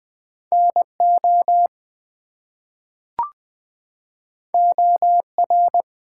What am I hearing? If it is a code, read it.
Morse code, heard as a keyed tone, about 20 wpm, 700 Hz. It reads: NO OR
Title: Two Letter Words - 20wpm